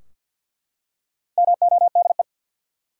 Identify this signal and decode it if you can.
Morse code, heard as a keyed tone, about 52 wpm, 700 Hz. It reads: MODE
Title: Four Letter Words - 50wpm